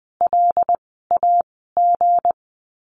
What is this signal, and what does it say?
Morse code, heard as a keyed tone, about 20 wpm, spaced as for 15 wpm, 700 Hz. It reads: LAG
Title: 160 Lesson 26f - Letter g - Words 20wpm